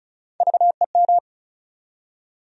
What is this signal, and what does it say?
Morse code, heard as a keyed tone, about 36 wpm, 700 Hz. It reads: VEM
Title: USA Call Sign Postfixes - 3 Letters 35wpm